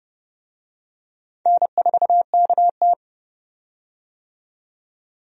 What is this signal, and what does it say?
Morse code, heard as a keyed tone, about 32 wpm, 700 Hz. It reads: N4KT
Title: USA Call Signs - 4 Characters - 30wpm